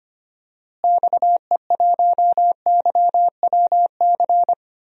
Morse code, heard as a keyed tone, about 25 wpm, 700 Hz. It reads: XE1YWC
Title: DXCC-100 Call Signs - 6 Characters - Repeated 3x - 25wpm